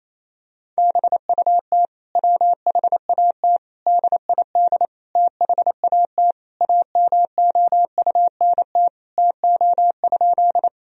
Morse code, heard as a keyed tone, about 27 wpm, 700 Hz. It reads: BUT WHAT DID THAT AMOUNT TO?
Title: Sentences from Top 1000 Words - Review Part 4 - 28wpm